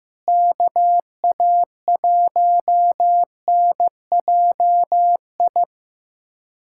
Morse code, heard as a keyed tone, about 15 wpm, 700 Hz. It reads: KA1NJI
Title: USA Callsign DE Callsign - 6 Characters - 15wpm